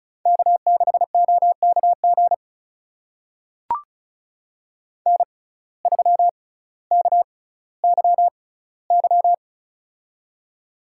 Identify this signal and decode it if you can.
Morse code, heard as a keyed tone, about 36 wpm, spaced as for 18 wpm, 700 Hz. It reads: K6OKG N3KYY
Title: USA Call Signs - 6x Character Spacing - 35wpm